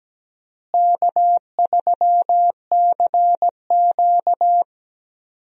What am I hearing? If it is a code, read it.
Morse code, heard as a keyed tone, about 17 wpm, 700 Hz. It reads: K3CQ
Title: USA Call Signs - 4 Characters - Repeated 3x - 17wpm